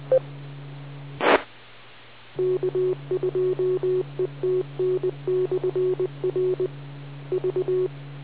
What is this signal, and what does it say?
Morse code, heard as a keyed tone, about 20 wpm, 370 Hz. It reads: T K2ETN/R V